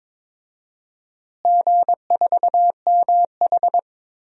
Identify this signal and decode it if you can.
Morse code, heard as a keyed tone, about 22 wpm, 700 Hz. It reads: G4MH
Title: DXCC-100 Call Signs - 4 Characters - Repeated 2x - 22wpm